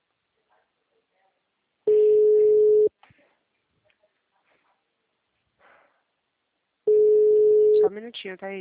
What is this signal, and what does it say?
Ringback tone (Europe, 425 Hz), with speech.